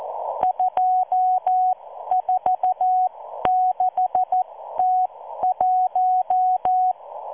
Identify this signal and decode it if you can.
Morse code, heard as a keyed tone, about 14 wpm, spaced as for 12 wpm, 740 Hz. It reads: 246T1